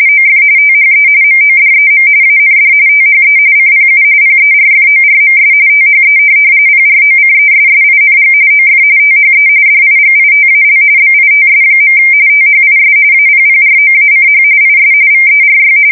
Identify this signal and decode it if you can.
RTTY, about 45.45 baud. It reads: OF THE MODEM AND THE CPU PERFORMS THE PROCESSING OF THE DIGITAL BITS. THIS APPROACH IS VERY
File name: fsk170 / teletype-RTTY